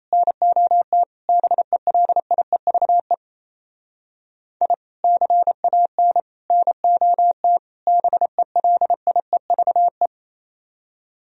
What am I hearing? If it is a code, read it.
Morse code, heard as a keyed tone, about 35 wpm, 700 Hz. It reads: NOT BELIEVE I CAN NOT BELIEVE
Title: Speed-Racing - Sets of 4 Words 28wpm